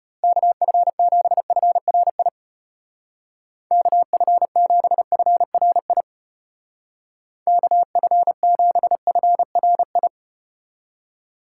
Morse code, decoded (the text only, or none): KF7FRI KF7FRI KF7FRI